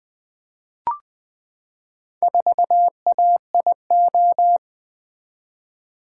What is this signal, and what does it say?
Morse code, heard as a keyed tone, about 20 wpm, 700 Hz. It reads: 4AIO
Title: Four Letter-Number Combinations - 20wpm